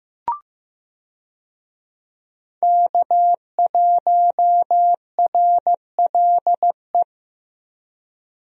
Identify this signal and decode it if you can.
Morse code, heard as a keyed tone, about 15 wpm, 700 Hz. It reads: K1RLE
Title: USA Call Signs - 5 Characters - Repeated 3x - 15wpm